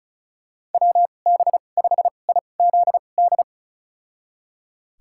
Morse code, heard as a keyed tone, about 36 wpm, spaced as for 25 wpm, 700 Hz. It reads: WB5IZD
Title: USA Call Signs - 2x Character Spacing - 35wpm